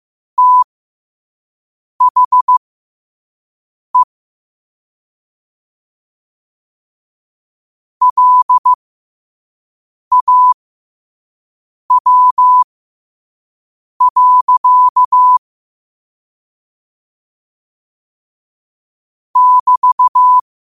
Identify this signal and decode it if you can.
Morse code, heard as a keyed tone, about 15 wpm, spaced as for 5 wpm, 1000 Hz. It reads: THE LAW. <BT>
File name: qotdfiles / 5wpm